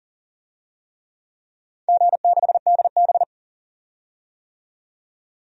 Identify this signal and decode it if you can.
Morse code, heard as a keyed tone, about 41 wpm, 700 Hz. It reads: G6DB
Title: DXCC-100 Call Signs - 4 Characters - 40wpm